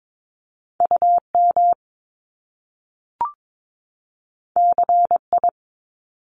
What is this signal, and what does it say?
Morse code, heard as a keyed tone, about 22 wpm, 700 Hz. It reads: UM CI
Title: Top Two Letter Combinations in Words 22wpm